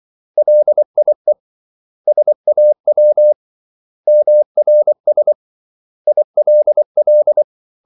Morse code, heard as a keyed tone, about 24 wpm, 600 Hz. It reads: LIE SAW MRS ILL
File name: CW / Practice / 500MostCommonEnglishWords24